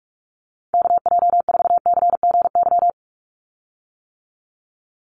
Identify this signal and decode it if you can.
Morse code, heard as a keyed tone, about 44 wpm, 700 Hz. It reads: KJ4CGY
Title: USA Call Signs 45wpm